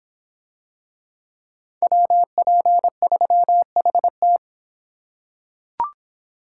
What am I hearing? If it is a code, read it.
Morse code, heard as a keyed tone, about 26 wpm, 700 Hz. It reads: WP3HT